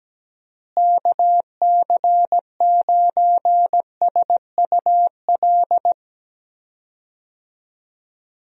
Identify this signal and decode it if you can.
Morse code, heard as a keyed tone, about 17 wpm, 700 Hz. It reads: KC9SUL
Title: USA Call Signs - 6 Characters - Repeated 2x - 17wpm